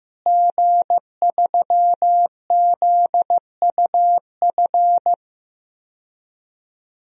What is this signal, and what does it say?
Morse code, heard as a keyed tone, about 15 wpm, 700 Hz. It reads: G3ZUF
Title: DXCC-100 Call Signs - 5 Characters - Repeated 2x - 15wpm